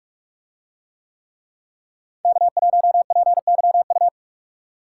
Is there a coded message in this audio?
Yes: Morse code from 2.2 to 4.1 s, heard as a keyed tone, about 43 wpm, 700 Hz. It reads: K1PYU